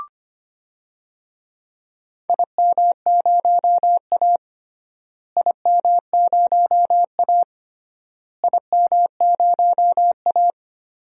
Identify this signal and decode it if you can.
Morse code, heard as a keyed tone, about 25 wpm, 700 Hz. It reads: IM0A IM0A IM0A
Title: DXCC-100 Call Signs - 4 Characters - Repeated 3x - 25wpm